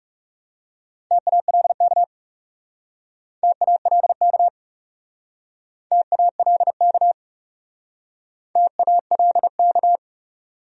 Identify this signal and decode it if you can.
Morse code, heard as a keyed tone, about 40 wpm, 700 Hz. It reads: TALK TALK TALK TALK